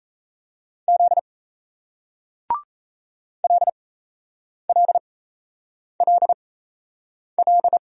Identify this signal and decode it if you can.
Morse code, heard as a keyed tone, about 38 wpm, spaced as for 8 wpm, 700 Hz. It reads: Z LLLL